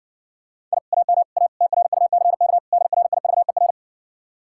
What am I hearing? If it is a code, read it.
Morse code, heard as a keyed tone, about 70 wpm, 700 Hz. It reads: I AM A TRUCK DRIVER